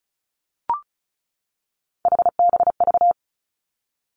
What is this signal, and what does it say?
Morse code, heard as a keyed tone, about 36 wpm, 700 Hz. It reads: HBV
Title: Three Challenging Letter-Number Combinations 35wpm